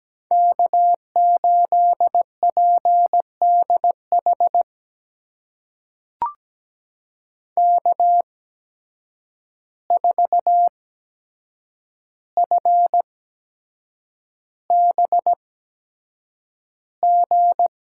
Morse code, heard as a keyed tone, about 17 wpm, spaced as for 7 wpm, 700 Hz. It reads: K8PDH K4FBG
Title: USA Call Signs - 5 Characters - 8x Character Spacing - 17wpm